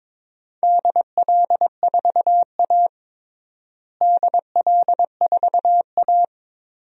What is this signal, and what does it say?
Morse code, heard as a keyed tone, about 22 wpm, 700 Hz. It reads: DL4A DL4A
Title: DXCC-100 Call Signs - 4 Characters - Repeated 2x - 22wpm